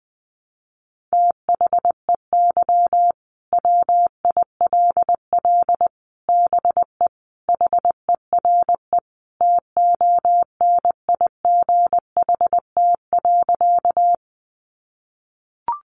Morse code, heard as a keyed tone, about 20 wpm, 700 Hz. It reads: THEY WILL BE HERE TONIGHT.
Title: Sentences from Top 600 Words 20wpm